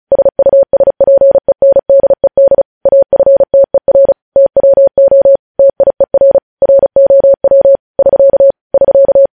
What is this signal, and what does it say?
Morse code, heard as a keyed tone, about 36 wpm, 550 Hz. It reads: SUSPENDED AFTER TWO TIER ROW <SK> <SK>
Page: News Headlines in Morse Code at 35 WPM